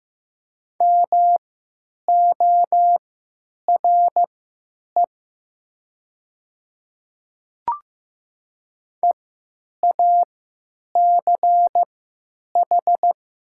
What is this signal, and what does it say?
Morse code, heard as a keyed tone, about 15 wpm, spaced as for 9 wpm, 700 Hz. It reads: MORE EACH
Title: 3x Character Spacing - Top 100 Words - 15wpm